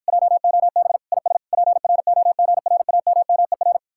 Morse code, heard as a keyed tone, about 52 wpm, 700 Hz. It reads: JOB IS PROGRAMMER